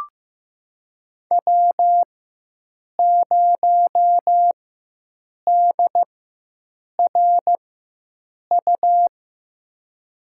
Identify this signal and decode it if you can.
Morse code, heard as a keyed tone, about 15 wpm, spaced as for 7 wpm, 700 Hz. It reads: W0DRU
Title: USA Call Signs - 4x Character Spacing - 15wpm